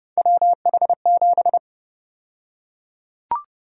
Morse code, heard as a keyed tone, about 30 wpm, 700 Hz. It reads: WH7
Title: USA Call Sign Prefixes - Repeated 2x - 30wpm